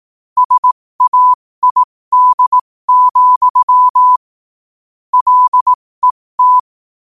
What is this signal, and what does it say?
Morse code, heard as a keyed tone, about 18 wpm, spaced as for 15 wpm, 1000 Hz. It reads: SAID, LET
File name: qotdfiles / 15wpm